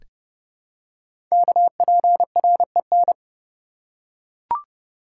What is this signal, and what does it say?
Morse code, heard as a keyed tone, about 30 wpm, 690 Hz. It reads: KPREN
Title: Code Groups - 5 Letters - 30wpm